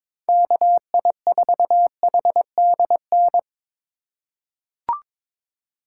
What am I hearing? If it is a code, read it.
Morse code, heard as a keyed tone, about 22 wpm, 700 Hz. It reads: KI4HDN
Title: USA Call Signs - 6 Characters - 4x Character Spacing - 22wpm